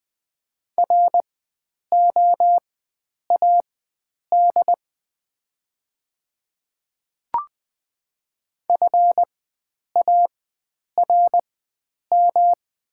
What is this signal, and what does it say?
Morse code, heard as a keyed tone, about 20 wpm, spaced as for 9 wpm, 700 Hz. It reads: ROAD FARM
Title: 4x Character Spacing - Top 500 Words - 20wpm